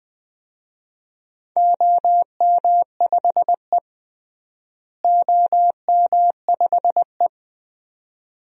Morse code, heard as a keyed tone, about 20 wpm, 700 Hz. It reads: OM5E OM5E
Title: DXCC-100 Call Signs - 4 Characters - Repeated 2x - 20wpm